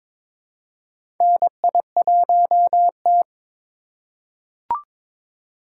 Morse code, heard as a keyed tone, about 22 wpm, 700 Hz. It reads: NI1T
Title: USA Call Signs - 4 Characters - 4x Character Spacing - 22wpm